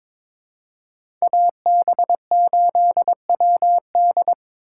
Morse code, heard as a keyed tone, about 21 wpm, 700 Hz. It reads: AB8WD
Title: USA Call Signs - 5 Characters - 3x Character Spacing - 22wpm